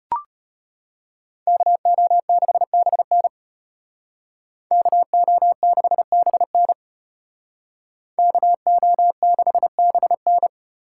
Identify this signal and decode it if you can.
Morse code, heard as a keyed tone, about 31 wpm, 700 Hz. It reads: KO6BN KO6BN KO6BN
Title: Speed Racing - USA 2x2 Call Signs 25wpm